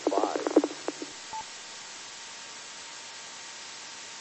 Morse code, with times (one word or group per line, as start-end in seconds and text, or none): none